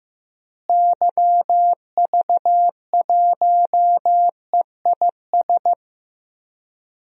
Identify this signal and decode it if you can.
Morse code, heard as a keyed tone, about 15 wpm, 700 Hz. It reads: YV1EIS
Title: DXCC-100 Call Signs - 6 Characters - Repeated 3x - 15wpm